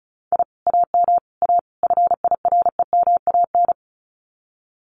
Morse code, heard as a keyed tone, about 36 wpm, 700 Hz. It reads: I AM A FIREMAN
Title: Common QSO Phrases Encore 35wpm